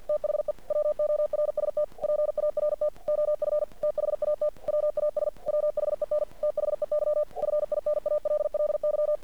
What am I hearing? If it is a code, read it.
Morse code, heard as a keyed tone, about 50 wpm, 600 Hz. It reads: THE WORST PART OF THAT WAS WHEN THEY FINALLY
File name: archived / CRSnet